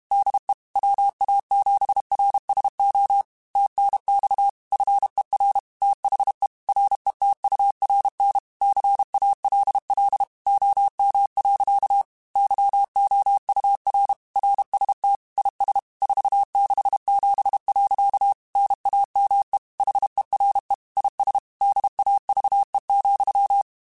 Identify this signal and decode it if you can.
Morse code, heard as a keyed tone, about 32 wpm, 800 Hz. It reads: DE WA7RSO TNX FER THE RETURN CALL OM. YOUR RST IS 467. NAME HERE IS DAVE,
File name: QSOs / QSOs-mp3s / kc7sac@30